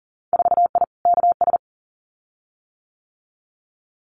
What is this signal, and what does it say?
Morse code, heard as a keyed tone, about 40 wpm, 700 Hz. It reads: 4I KS
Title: Winter Field Day Exchanges Encore 40wpm